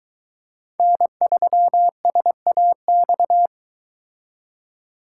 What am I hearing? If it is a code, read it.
Morse code, heard as a keyed tone, about 22 wpm, 700 Hz. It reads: N3SAX